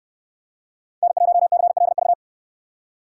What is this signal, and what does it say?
Morse code, heard as a keyed tone, about 68 wpm, 700 Hz. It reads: N1CLV